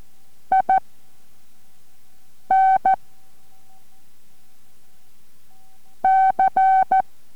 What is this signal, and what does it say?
Morse code, heard as a keyed tone, about 14 wpm, spaced as for 4 wpm, 760 Hz. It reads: IN C
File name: MorseCode-5WPM 3